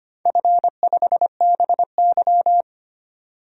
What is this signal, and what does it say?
Morse code, heard as a keyed tone, about 25 wpm, 700 Hz. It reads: F5BY
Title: DXCC-100 Call Signs - 4 Characters - 25wpm